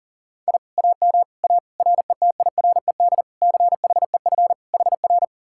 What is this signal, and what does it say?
Morse code, heard as a keyed tone, about 51 wpm, 700 Hz. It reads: I AM A RETIRED CHEF HR